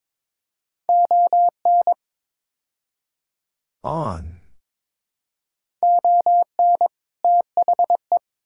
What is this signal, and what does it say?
Morse code, heard as a keyed tone, about 21 wpm, 700 Hz. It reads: ON ON THE
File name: practice / 22wpm / Build A Phrase - 5-Word Phrases - 22wpm